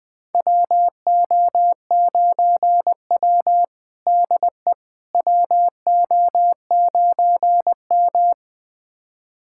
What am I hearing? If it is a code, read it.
Morse code, heard as a keyed tone, about 20 wpm, 700 Hz. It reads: WO9W DE WO9M
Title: USA Callsign DE Callsign - 4 Characters - 20wpm